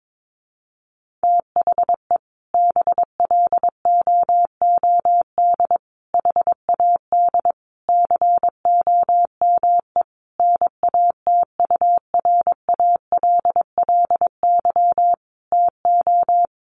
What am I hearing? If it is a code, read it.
Morse code, heard as a keyed tone, about 22 wpm, 700 Hz. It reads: THE BLOOD HAD COME NATURALLY TO
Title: Sentences from Top 1300 Words 22wpm